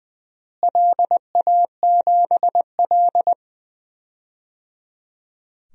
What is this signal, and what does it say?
Morse code, heard as a keyed tone, about 20 wpm, 700 Hz. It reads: LA7L
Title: DXCC-100 Call Signs - 4 Characters - Repeated 3x - 20wpm